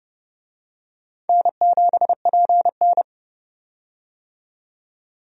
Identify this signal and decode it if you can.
Morse code, heard as a keyed tone, about 31 wpm, 700 Hz. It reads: N7PN
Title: USA Call Signs - 4 Characters - 30wpm